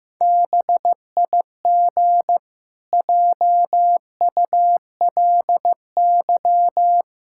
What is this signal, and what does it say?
Morse code, heard as a keyed tone, about 15 wpm, 700 Hz. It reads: BIG JULY
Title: Pangram Challenge v2 - 15wpm